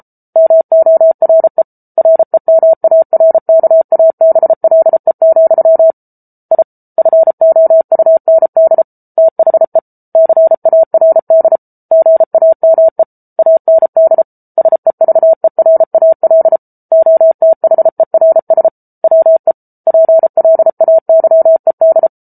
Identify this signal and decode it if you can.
Morse code, heard as a keyed tone, about 34 wpm, 650 Hz. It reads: MORE REMARKABLE, I FOUND THE CARD GAME AND SEVERAL OTHERS WE PLAYED